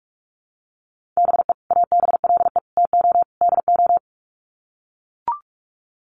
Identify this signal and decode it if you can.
Morse code, heard as a keyed tone, about 44 wpm, 700 Hz. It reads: BE ABLE TO DO
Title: Sets of 4 Words 45wpm